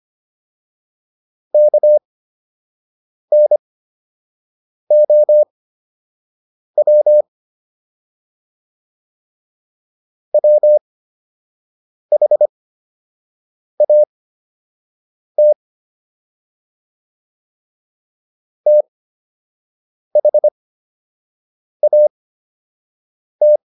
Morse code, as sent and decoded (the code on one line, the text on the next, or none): -.- -. --- .-- / .-- .... .- - / - .... .- -
KNOW WHAT THAT